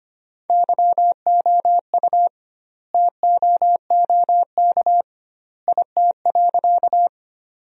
Morse code, heard as a keyed tone, about 25 wpm, 700 Hz. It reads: YOU TOOK IT.
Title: 2x Word-Spacing - Sentences from Top 300 Words - 25wpm